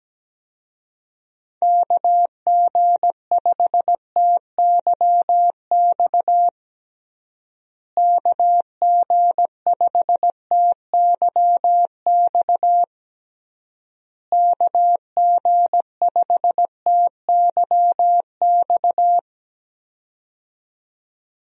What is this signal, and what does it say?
Morse code, heard as a keyed tone, about 17 wpm, 700 Hz. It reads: KG5TYX KG5TYX KG5TYX
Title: USA Call Signs - 6 Characters - Repeated 3x - 17wpm